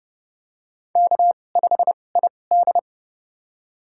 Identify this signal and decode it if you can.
Morse code, heard as a keyed tone, about 29 wpm, spaced as for 21 wpm, 700 Hz. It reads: K5ID
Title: USA Call Signs - 4 Characters - 2x Character Spacing - 30wpm